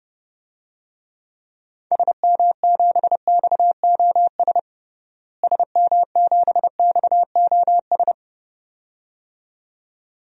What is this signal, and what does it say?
Morse code, heard as a keyed tone, about 30 wpm, 700 Hz. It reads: SM7XOS SM7XOS
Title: DXCC-100 Call Signs - 6 Characters - Repeated 2x - 30wpm